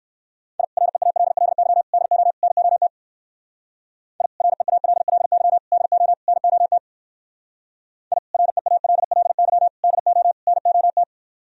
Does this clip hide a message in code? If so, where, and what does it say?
Yes: Morse code from 0.6 to 11.0 s, heard as a keyed tone, about 59 wpm, 700 Hz. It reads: I REALLY DO NOT I REALLY DO NOT I REALLY DO NOT